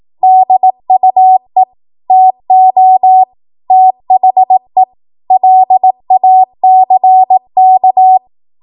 Morse code, heard as a keyed tone, about 18 wpm, 750 Hz. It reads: DUE TO THE LACK
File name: Morse / 18WPM